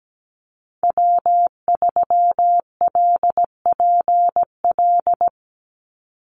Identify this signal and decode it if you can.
Morse code, heard as a keyed tone, about 17 wpm, 700 Hz. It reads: W3LPL
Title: Call Signs Repeated 3x - 17wpm